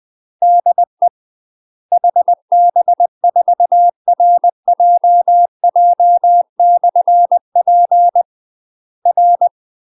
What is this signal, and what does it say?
Morse code, heard as a keyed tone, about 20 wpm, 700 Hz. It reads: DE HB4RJJ/P R
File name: UK MASTER QSOs TRAINING / 20WPM / UK MASTER QSOs 0010